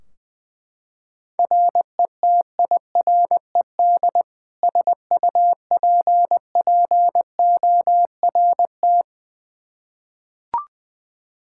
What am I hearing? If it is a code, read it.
Morse code, heard as a keyed tone, about 20 wpm, 700 Hz. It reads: RETIRED SUPPORT
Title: Practice QSO - Exchange 3 - More Info - 20wpm